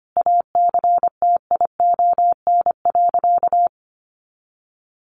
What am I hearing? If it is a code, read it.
Morse code, heard as a keyed tone, about 25 wpm, 700 Hz. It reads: ACTION.
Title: Sentences from Top 1300 Words 25wpm